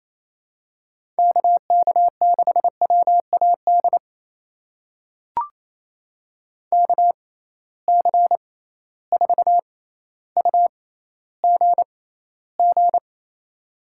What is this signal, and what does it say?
Morse code, heard as a keyed tone, about 28 wpm, spaced as for 14 wpm, 700 Hz. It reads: KK6WAD KC4UGG
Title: USA Call Signs - 6 Characters - 6x Character Spacing - 28wpm